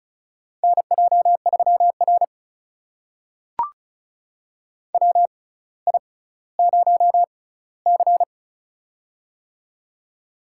Morse code, heard as a keyed tone, about 36 wpm, spaced as for 18 wpm, 700 Hz. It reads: NJ3R WI0C
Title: USA Call Signs - 4 Characters - 6x Character Spacing - 35wpm